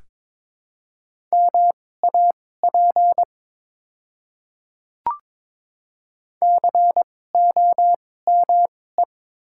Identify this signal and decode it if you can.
Morse code, heard as a keyed tone, about 22 wpm, spaced as for 16 wpm, 700 Hz. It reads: MAP COME
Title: 2x Character Spacing - Top 500 Words - 22wpm